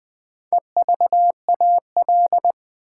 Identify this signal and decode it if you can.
Morse code, heard as a keyed tone, about 20 wpm, 700 Hz. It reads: EVAL
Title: Build A Word - 2 Letters At A Time - 8-Letter Words - 20wpm